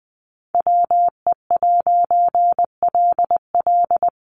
Morse code, heard as a keyed tone, about 20 wpm, 700 Hz. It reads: WE'LL
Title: A Sound Of Thunder 20wpm